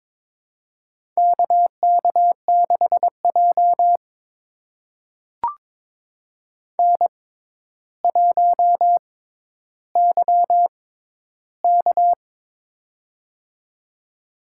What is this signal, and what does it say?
Morse code, heard as a keyed tone, about 22 wpm, spaced as for 11 wpm, 700 Hz. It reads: KK6J N1YK
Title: USA Call Signs - 4 Characters - 6x Character Spacing - 22wpm